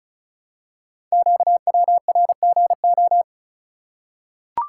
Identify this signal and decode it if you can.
Morse code, heard as a keyed tone, about 36 wpm, 700 Hz. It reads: QWRGO
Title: Code Groups - 5 Letters - 35wpm